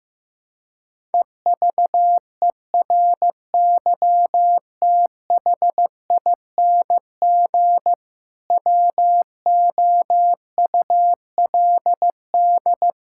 Morse code, heard as a keyed tone, about 16 wpm, 700 Hz. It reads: EVERYTHING WOULD
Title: Sentences from Top 1000 Words - Review Part 5 - 15wpm